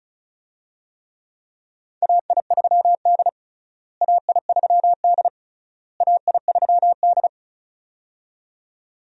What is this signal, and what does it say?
Morse code, heard as a keyed tone, about 36 wpm, 700 Hz. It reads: AI3D AI3D AI3D
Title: USA Call Signs - 4 Characters - Repeated 3x - 35wpm